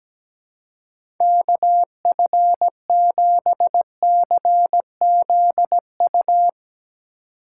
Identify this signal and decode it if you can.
Morse code, heard as a keyed tone, about 17 wpm, 700 Hz. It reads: KF7CZU
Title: Speed Racing - USA 2x3 Call Signs 17wpm